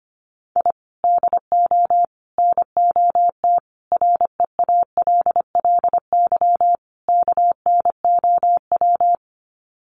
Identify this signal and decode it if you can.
Morse code, heard as a keyed tone, about 25 wpm, 700 Hz. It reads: I DO NOT REALLY KNOW
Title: Sets of 5 Words Encore 25wpm